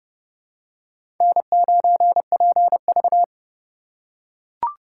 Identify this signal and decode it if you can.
Morse code, heard as a keyed tone, about 28 wpm, 700 Hz. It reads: N9PV
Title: USA Call Signs - 4 Characters - 3x Character Spacing - 30wpm